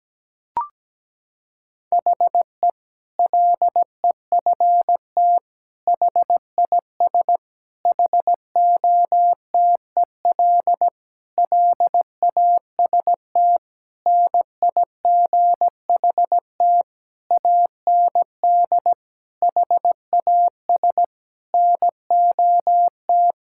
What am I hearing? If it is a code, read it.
Morse code, heard as a keyed tone, about 17 wpm, 700 Hz. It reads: HE LEFT HIS HOTEL LAST NIGHT AND HAS NOT
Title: Sentences from Top 1600 Words 17wpm